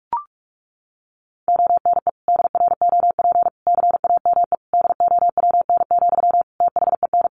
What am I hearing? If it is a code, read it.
Morse code, heard as a keyed tone, about 44 wpm, 700 Hz. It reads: ONE DROP CAME DOWN, THEN
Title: Sentences from Top 1400 Words 45wpm